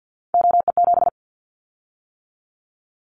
Morse code, heard as a keyed tone, about 50 wpm, 700 Hz. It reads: OE7
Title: Callsign Prefixes for Top 100 DXCC - 50wpm